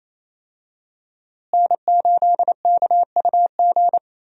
Morse code, heard as a keyed tone, about 27 wpm, 700 Hz. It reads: N8KUG
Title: USA Call Signs - 5 Characters - Repeated 3x - 28wpm